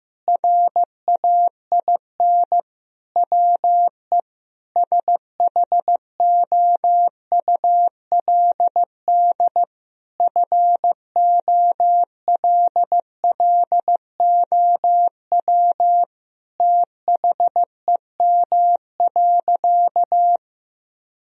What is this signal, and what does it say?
Morse code, heard as a keyed tone, about 16 wpm, 700 Hz. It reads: RAIN WE SHOULD FOLLOW THEM.
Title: Sentences from Top 1000 Words - Review Part 1 - 15wpm